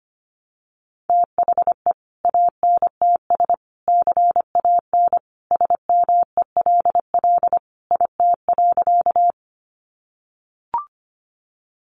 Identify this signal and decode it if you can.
Morse code, heard as a keyed tone, about 25 wpm, 700 Hz. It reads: THE ANTS CAN SMELL IT.
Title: Additional Easy Phrases 25wpm